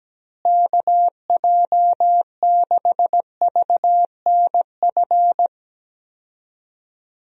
Morse code, heard as a keyed tone, about 17 wpm, 700 Hz. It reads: KJ6VNF
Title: USA Call Signs - 6 Characters - Repeated 3x - 17wpm